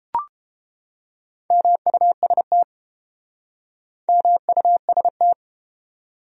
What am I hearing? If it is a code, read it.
Morse code, heard as a keyed tone, about 32 wpm, 700 Hz. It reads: MUST MUST